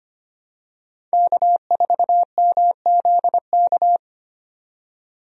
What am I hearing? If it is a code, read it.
Morse code, heard as a keyed tone, about 25 wpm, 700 Hz. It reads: K4MZK
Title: USA Call Signs - 5 Characters - 25wpm